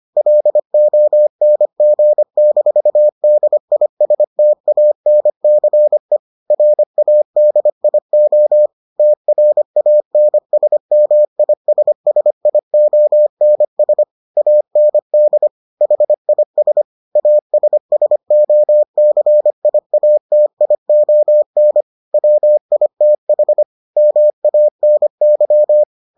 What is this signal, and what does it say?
Morse code, heard as a keyed tone, about 25 wpm, 600 Hz. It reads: LONG-DISTANCE RADIO TRANSMISSIONS AND HIS ASSOCIATION WITH MANY